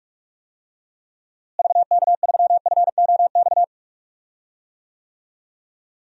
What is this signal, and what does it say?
Morse code, heard as a keyed tone, about 46 wpm, 700 Hz. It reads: VK3FOX
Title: DXCC-100 Call Signs - 6 Characters - Repeated 3x - 45wpm